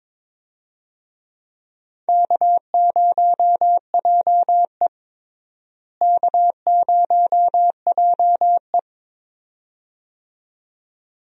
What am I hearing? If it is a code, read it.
Morse code, heard as a keyed tone, about 22 wpm, 700 Hz. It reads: K0JE K0JE
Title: USA Call Signs - 4 Characters - Repeated 2x - 22wpm